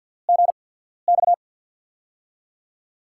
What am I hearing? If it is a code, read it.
Morse code, heard as a keyed tone, about 52 wpm, spaced as for 14 wpm, 700 Hz. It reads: GX
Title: USA Call Signs - 6 Characters - 8x Character Spacing - 50wpm